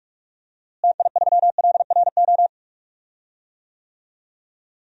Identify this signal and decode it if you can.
Morse code, heard as a keyed tone, about 46 wpm, 700 Hz. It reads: TI3LRO
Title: DXCC-100 Call Signs - 6 Characters - Repeated 2x - 45wpm